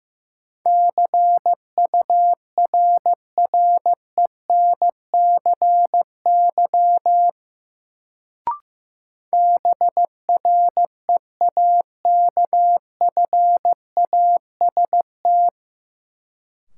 Morse code, heard as a keyed tone, about 15 wpm, 700 Hz. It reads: CURRENCY BREAKFAST
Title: Sentences from Top 1500 Words - New Word Drill - 15wpm